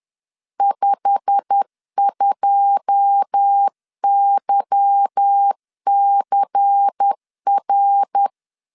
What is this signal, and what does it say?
Morse code, heard as a keyed tone, about 10 wpm, 800 Hz. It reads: 52YCR
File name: LEZIONI / 9 2 10wpm 2736